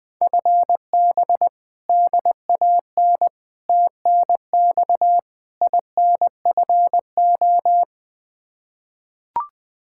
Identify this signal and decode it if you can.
Morse code, heard as a keyed tone, about 20 wpm, 700 Hz. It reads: FB DAN TNX INFO
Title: Practice QSO - Exchange 3 - More Info - 20wpm